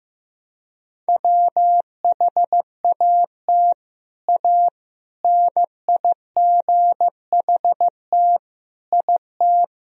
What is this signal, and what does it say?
Morse code, heard as a keyed tone, about 15 wpm, 700 Hz. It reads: WHAT A NIGHT IT
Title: Sentences from Top 300 Words and Abbreviations 15wpm